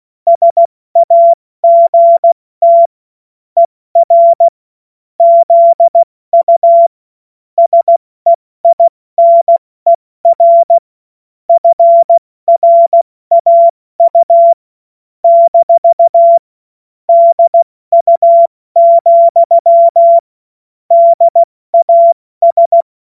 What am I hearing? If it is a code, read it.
Morse code, heard as a keyed tone, about 16 wpm, 680 Hz. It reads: SAGT ER ZU SEINER FRAU - DU, DAS
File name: Morsekurs / Witze / Witz 13. April 2021, 70 F80